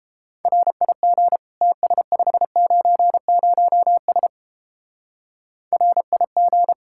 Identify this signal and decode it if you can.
Morse code, heard as a keyed tone, about 29 wpm, 700 Hz. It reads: RIG TS590S RIG